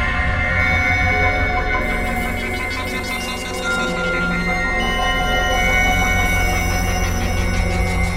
Dial-up modem handshake.